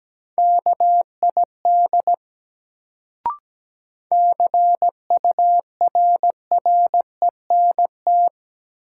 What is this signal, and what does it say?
Morse code, heard as a keyed tone, about 17 wpm, 700 Hz. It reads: KID CURRENT
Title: Sentences from Top 1600 Words - New Word Drill Alt - 17wpm